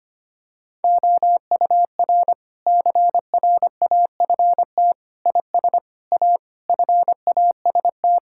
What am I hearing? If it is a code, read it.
Morse code, heard as a keyed tone, about 25 wpm, 700 Hz. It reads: OUR CRAFT IS A FAST
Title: Sentences from Top 1000 Words - Review Part 6 - 25wpm